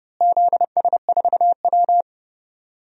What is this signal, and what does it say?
Morse code, heard as a keyed tone, about 30 wpm, 700 Hz. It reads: ZS4W
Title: DXCC-100 Call Signs - 4 Characters - Repeated 3x - 30wpm